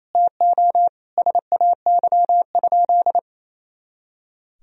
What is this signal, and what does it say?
Morse code, heard as a keyed tone, about 28 wpm, 700 Hz. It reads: TO SAY?
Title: Sentences from Top 200 Words and Abbreviations 28wpm